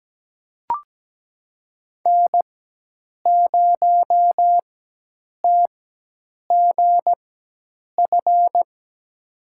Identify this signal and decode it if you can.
Morse code, heard as a keyed tone, about 17 wpm, spaced as for 8 wpm, 700 Hz. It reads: N0TGF
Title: USA Call Signs - 5 Characters - 4x Character Spacing - 17wpm